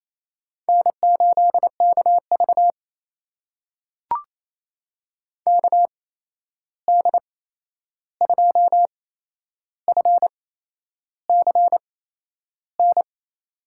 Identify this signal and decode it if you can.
Morse code, heard as a keyed tone, about 28 wpm, spaced as for 10 wpm, 700 Hz. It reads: N8KV KD2FCN
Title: USA Call Signs - 8x Character Spacing - 28wpm